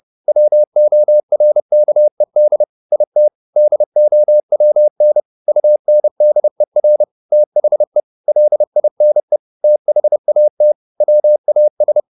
Morse code, heard as a keyed tone, about 30 wpm, 600 Hz. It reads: WORKED IT DOWN UNDER THE LINE THAT WAS
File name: the old man and the sea / the old man and the sea 1 30wpm